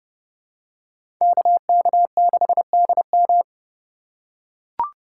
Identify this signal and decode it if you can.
Morse code, heard as a keyed tone, about 30 wpm, 700 Hz. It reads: KK6DM
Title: USA Call Signs - 5 Characters - 4x Character Spacing - 30wpm